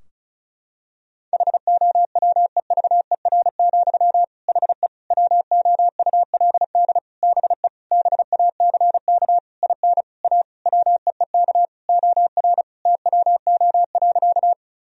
Morse code, heard as a keyed tone, about 36 wpm, 700 Hz. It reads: HOWEVER, HE WOULD BE BACK IN A WEEK OR TWO.
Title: Sentences from Top 1000 Words - Review Part 2 - 35wpm